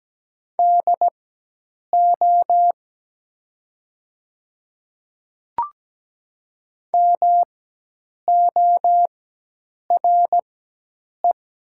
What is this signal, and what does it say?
Morse code, heard as a keyed tone, about 17 wpm, spaced as for 8 wpm, 700 Hz. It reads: DO MORE
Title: 4x Character Spacing - Top 100 Words - 17wpm